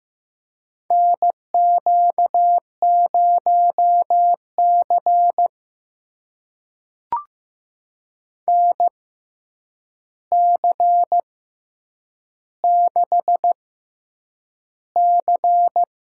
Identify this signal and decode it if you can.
Morse code, heard as a keyed tone, about 15 wpm, spaced as for 8 wpm, 700 Hz. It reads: NQ0C NC6C
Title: USA Call Signs - 4 Characters - 6x Character Spacing - 15wpm